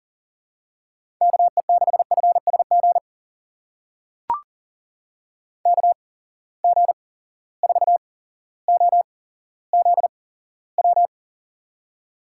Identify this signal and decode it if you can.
Morse code, heard as a keyed tone, about 41 wpm, spaced as for 17 wpm, 700 Hz. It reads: KE6FSG KG4OZW
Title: USA Call Signs - 6 Characters - 8x Character Spacing - 40wpm